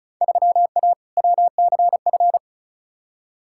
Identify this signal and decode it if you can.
Morse code, heard as a keyed tone, about 34 wpm, 700 Hz. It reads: 3A WCF
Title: Field Day Exchanges Encore - 35wpm